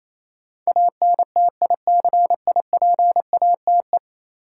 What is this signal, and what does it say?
Morse code, heard as a keyed tone, about 28 wpm, 700 Hz. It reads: ANTICIPATE
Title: Sentences from Top 1500 Words - New Word Drill - 28wpm